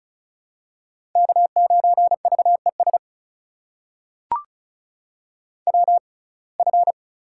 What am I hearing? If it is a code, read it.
Morse code, heard as a keyed tone, about 36 wpm, spaced as for 25 wpm, 700 Hz. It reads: K9VES WF
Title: USA Call Signs - 6x Character Spacing - 35wpm